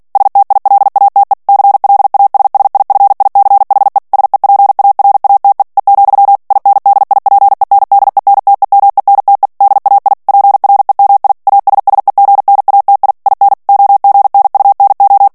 Morse code, heard as a keyed tone, about 48 wpm, 800 Hz. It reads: STILATE CLASSIFICHE SEPARATE E, INDIPENDENTEMENTE DAI PREMI ASSEGNATI IN OGNUNO
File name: ITALIEN texte clair 2 vitesse 45 mots minute